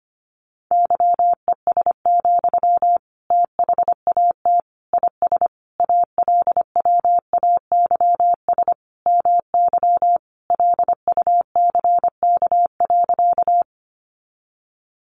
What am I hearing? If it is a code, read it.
Morse code, heard as a keyed tone, about 25 wpm, 700 Hz. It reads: YES, THAT IS ALWAYS MY LUCK.
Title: Sentences from Top 1500 Words 25wpm